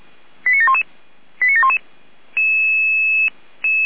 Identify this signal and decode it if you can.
Five-tone selective call (ZVEI), heard as quick runs of five tones, tones of about 70 ms. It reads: 89711 89711